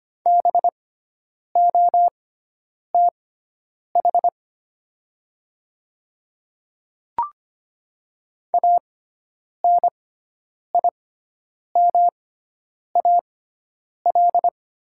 Morse code, heard as a keyed tone, about 25 wpm, spaced as for 9 wpm, 700 Hz. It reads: BOTH ANIMAL
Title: 6x Character Spacing - Top 300 Words - 25wpm